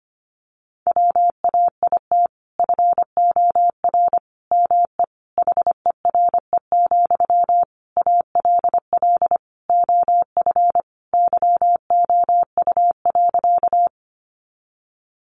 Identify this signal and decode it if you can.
Morse code, heard as a keyed tone, about 25 wpm, 700 Hz. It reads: WAIT FOR ME HERE, ALL OF YOU.
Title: Sentences from Top 500 Words 25wpm